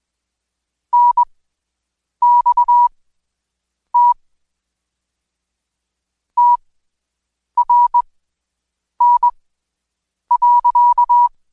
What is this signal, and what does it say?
Morse code, heard as a keyed tone, about 21 wpm, spaced as for 7 wpm, 970 Hz. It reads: NXT TRN.
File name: Code Quick / AudioDownld2 / Seg113-Day18-DayDrill 18